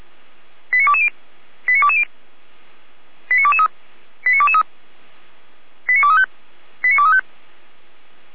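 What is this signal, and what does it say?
Five-tone selective call (ZVEI), heard as quick runs of five tones, tones of about 70 ms. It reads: 89220 89220 89223 89223 89235 89235
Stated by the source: ZVEI five-tone call received on an SDR.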